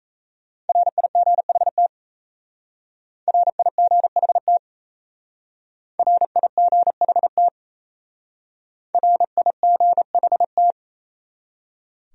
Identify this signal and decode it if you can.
Morse code, heard as a keyed tone, about 37 wpm, 700 Hz. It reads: RIGHT RIGHT RIGHT RIGHT